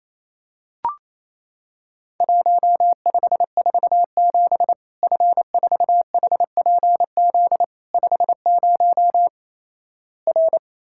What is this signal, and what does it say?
Morse code, heard as a keyed tone, about 28 wpm, 700 Hz. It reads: 1547 F4HPZ 50 R
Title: Worked All Europe DX Contest - QTC Exchange - 28wpm